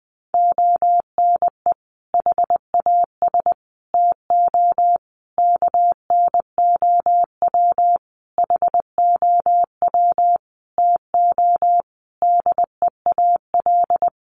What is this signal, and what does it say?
Morse code, heard as a keyed tone, about 20 wpm, 700 Hz. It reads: ONE HAS TO KNOW HOW TO DEAL
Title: Sentences from Top 800 Words 20wpm